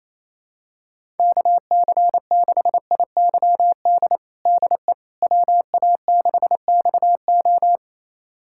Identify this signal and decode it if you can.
Morse code, heard as a keyed tone, about 28 wpm, 700 Hz. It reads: KC6IYD DE WA6XO
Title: USA Callsign DE Callsign 28wpm